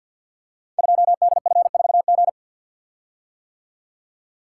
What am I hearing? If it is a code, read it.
Morse code, heard as a keyed tone, about 48 wpm, 700 Hz. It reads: 2DF4G
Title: Code Groups - 5 Letter:Numbers - 50wpm